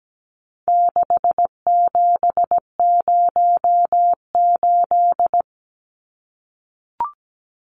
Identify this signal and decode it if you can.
Morse code, heard as a keyed tone, about 17 wpm, 700 Hz. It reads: 6708
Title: Numbers - 4-Digit 17wpm